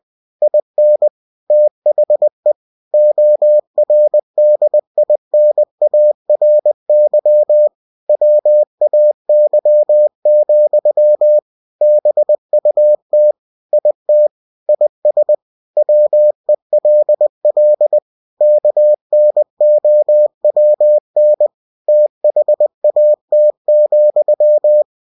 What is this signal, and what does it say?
Morse code, heard as a keyed tone, about 20 wpm, 600 Hz. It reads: IN THE ORDINARY WAY, BUT IT IS WELL KNOWN THAT,
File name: cw-books / winnie-the-pooh / winnie-20wpm-page-0062